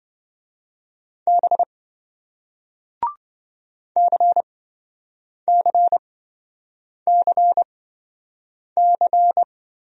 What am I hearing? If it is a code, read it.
Morse code, heard as a keyed tone, about 26 wpm, spaced as for 7 wpm, 700 Hz. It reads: B CCCC